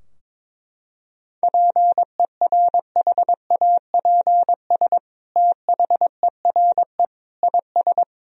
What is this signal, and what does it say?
Morse code, heard as a keyed tone, about 23 wpm, 700 Hz. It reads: PERHAPS THERE IS
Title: Sentences from Top 1000 Words - Review Part 4 - 22wpm